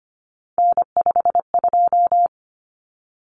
Morse code, heard as a keyed tone, about 25 wpm, 700 Hz. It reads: N52
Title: Common Cut Numbers - 3-Digit 25wpm